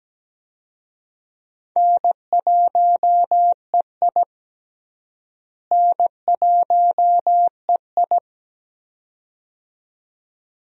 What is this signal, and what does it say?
Morse code, heard as a keyed tone, about 17 wpm, 700 Hz. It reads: N1EI N1EI
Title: USA Call Signs - 4 Characters - Repeated 2x - 17wpm